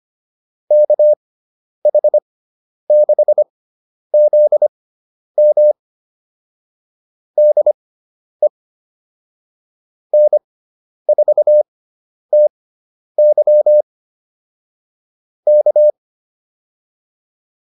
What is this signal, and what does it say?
Morse code, heard as a keyed tone, about 25 wpm, spaced as for 10 wpm, 600 Hz. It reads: KH6ZM DE N4TY K